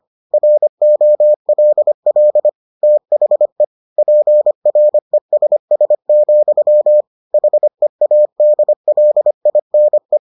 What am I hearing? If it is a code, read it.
Morse code, heard as a keyed tone, about 25 wpm, 600 Hz. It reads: ROLL THE PRESS, HEADLINE